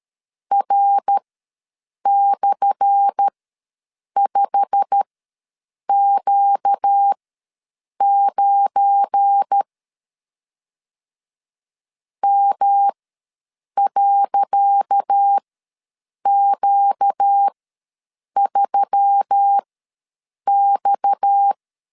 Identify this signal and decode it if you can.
Morse code, heard as a keyed tone, about 13 wpm, spaced as for 7 wpm, 800 Hz. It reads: R/5Q9 M.Q3X
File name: LEZIONI / 10 SIGNS / 10D COMA ws 12wpm 3618